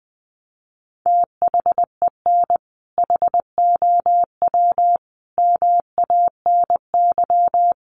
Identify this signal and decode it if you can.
Morse code, heard as a keyed tone, about 20 wpm, 700 Hz. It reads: THEN HOW MANY
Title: Sentences from Top 100 Words 20wpm